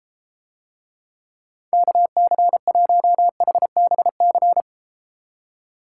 Morse code, heard as a keyed tone, about 32 wpm, 700 Hz. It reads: KC1HBC